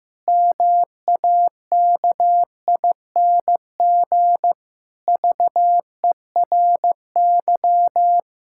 Morse code, heard as a keyed tone, about 16 wpm, 700 Hz. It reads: MAKING VERY
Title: Pangram Challenge v2 - 15wpm